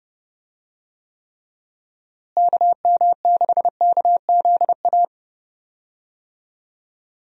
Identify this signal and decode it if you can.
Morse code, heard as a keyed tone, about 30 wpm, 700 Hz. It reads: KM6KZA
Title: USA Call Signs - 6 Characters - Repeated 2x - 30wpm